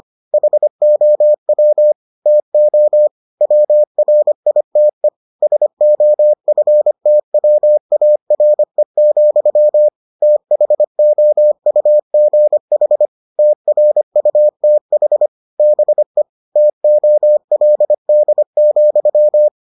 Morse code, heard as a keyed tone, about 25 wpm, 600 Hz. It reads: HOW TO WRITE SOFTWARE, THOUGH TRUTH BE TOLD,